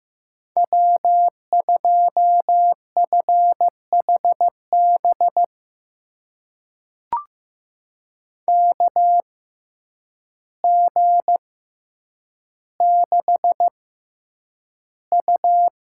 Morse code, heard as a keyed tone, about 15 wpm, spaced as for 8 wpm, 700 Hz. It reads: W2FHB KG6U
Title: USA Call Signs - 5 Characters - 6x Character Spacing - 15wpm